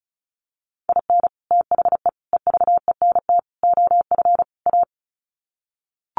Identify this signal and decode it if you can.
Morse code, heard as a keyed tone, about 34 wpm, 700 Hz. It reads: IN THE EVENT OF A
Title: Sets of 5 Words Encore 35wpm